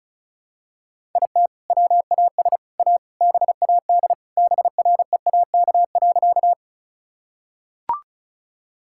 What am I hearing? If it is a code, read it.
Morse code, heard as a keyed tone, about 34 wpm, 700 Hz. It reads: IT WAS A BAD BREAK.
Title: Sentences from Top 1000 Words - Review Part 6 - 35wpm